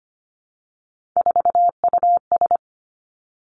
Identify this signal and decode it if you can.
Morse code, heard as a keyed tone, about 25 wpm, 700 Hz. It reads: 4US
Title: Three Challenging Letter-Number Combinations 25wpm